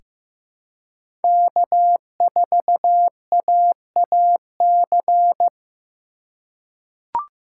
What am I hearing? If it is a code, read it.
Morse code, heard as a keyed tone, about 15 wpm, 700 Hz. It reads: K4AAC
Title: K1USN SST QSO Practice - 15wpm